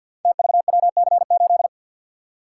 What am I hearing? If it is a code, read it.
Morse code, heard as a keyed tone, about 53 wpm, 700 Hz. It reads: TVWC8
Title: Code Groups - 5 Letter:Numbers - 50wpm